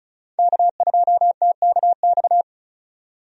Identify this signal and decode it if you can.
Morse code, heard as a keyed tone, about 36 wpm, 700 Hz. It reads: K2TKX
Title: USA Call Signs - 5 Characters - 35wpm